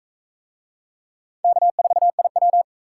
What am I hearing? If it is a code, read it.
Morse code, heard as a keyed tone, about 38 wpm, 700 Hz. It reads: K4IW